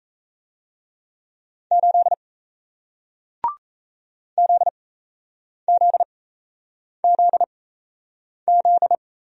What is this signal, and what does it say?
Morse code, heard as a keyed tone, about 38 wpm, spaced as for 8 wpm, 700 Hz. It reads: 8 ZZZZ